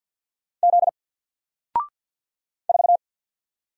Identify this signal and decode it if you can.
Morse code, heard as a keyed tone, about 48 wpm, 700 Hz. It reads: Z 4
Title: Single Letter-Number Alt 50wpm